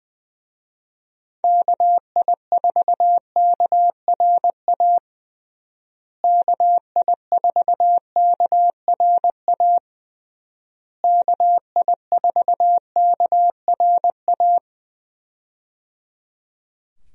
Morse code, heard as a keyed tone, about 20 wpm, 700 Hz. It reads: KI4KRA KI4KRA KI4KRA
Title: USA Call Signs - 6 Characters - Repeated 3x - 20wpm